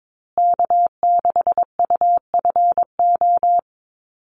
Morse code, heard as a keyed tone, about 22 wpm, 700 Hz. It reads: K6UFO
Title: Call Signs Repeated 3x - 22wpm